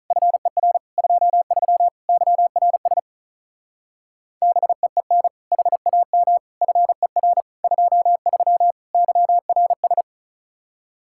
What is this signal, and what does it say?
Morse code, heard as a keyed tone, about 44 wpm, 700 Hz. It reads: FER 23 YRS BEEN HAM FER 23 YRS
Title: Speed-Racing Common QSO Phrases Encore - 35wpm